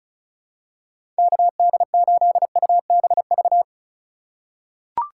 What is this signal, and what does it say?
Morse code, heard as a keyed tone, about 34 wpm, 700 Hz. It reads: KD8UBV
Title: USA Call Signs - 6 Characters - 4x Character Spacing - 35wpm